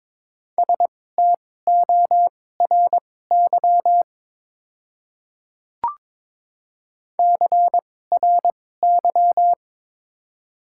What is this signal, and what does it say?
Morse code, heard as a keyed tone, about 22 wpm, spaced as for 16 wpm, 700 Hz. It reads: STORY CRY
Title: 2x Character Spacing - Top 500 Words - 22wpm